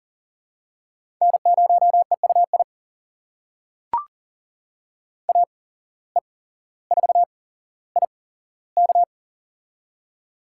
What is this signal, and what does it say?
Morse code, heard as a keyed tone, about 38 wpm, spaced as for 17 wpm, 700 Hz. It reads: N0EUI AE4IK
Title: USA Call Signs - 5 Characters - 8x Character Spacing - 40wpm